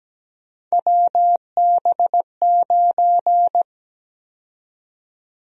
Morse code, heard as a keyed tone, about 17 wpm, 700 Hz. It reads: WB9
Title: USA Call Sign Prefixes - Repeated 2x - 17wpm